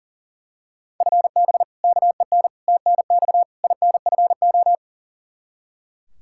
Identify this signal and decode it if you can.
Morse code, heard as a keyed tone, about 40 wpm, 700 Hz. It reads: FB KEN TNX INFO
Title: Practice QSO - Exchange 3 - More Info - 40wpm